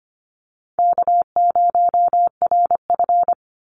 Morse code, heard as a keyed tone, about 25 wpm, 700 Hz. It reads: K0RF
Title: Call Signs - 25wpm